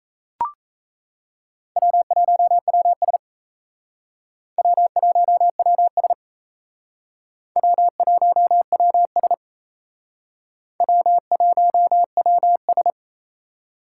Morse code, heard as a keyed tone, about 36 wpm, 700 Hz. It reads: W1WS W1WS W1WS W1WS